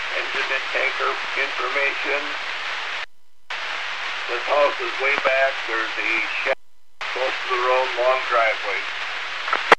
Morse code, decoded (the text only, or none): none